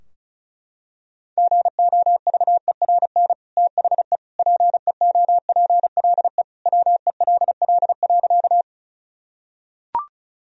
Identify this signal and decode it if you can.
Morse code, heard as a keyed tone, about 36 wpm, 700 Hz. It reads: GOVERN THE PEOPLE WELL.
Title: Sentences from Top 1000 Words - Review Part 2 - 35wpm